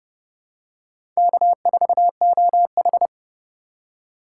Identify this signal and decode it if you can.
Morse code, heard as a keyed tone, about 30 wpm, 700 Hz. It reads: K4OH
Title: USA Call Signs - 4 Characters - 4x Character Spacing - 30wpm